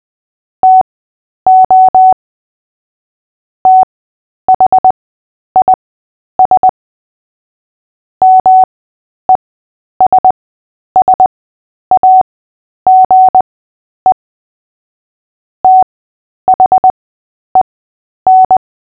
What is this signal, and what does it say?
Morse code, heard as a keyed tone, about 20 wpm, spaced as for 10 wpm, 750 Hz. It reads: TO THIS MESSAGE THEN
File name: cw / fists / Short-Story-8-10wpm